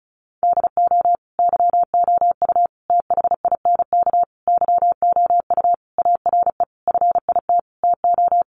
Morse code, heard as a keyed tone, about 34 wpm, 700 Hz. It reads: DO YOU THINK YOU ARE FIT TO
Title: Sentences from Top 1200 Words 35wpm